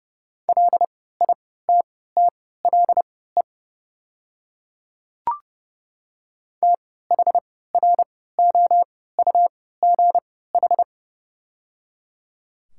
Morse code, heard as a keyed tone, about 30 wpm, spaced as for 17 wpm, 690 Hz. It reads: LITTLE THROUGH
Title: 3x Character Spacing - Top 500 Words - 30wpm